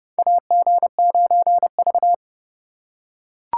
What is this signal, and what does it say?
Morse code, heard as a keyed tone, about 30 wpm, 700 Hz. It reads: AG9V
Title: USA Call Signs - 4 Characters - 3x Character Spacing - 30wpm